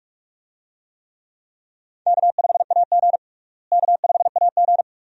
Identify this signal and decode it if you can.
Morse code, heard as a keyed tone, about 47 wpm, 700 Hz. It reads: K5AG K5AG
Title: USA Call Signs - 4 Characters - Repeated 2x - 45wpm